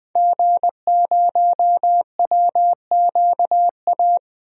Morse code, heard as a keyed tone, about 20 wpm, 700 Hz. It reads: G0WQA
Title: DXCC-100 Call Signs - 5 Characters - Repeated 2x - 20wpm